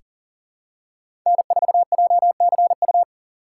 Morse code, heard as a keyed tone, about 40 wpm, 700 Hz. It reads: N4JCU
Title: USA Call Signs - 5 Characters - 4x Character Spacing - 40wpm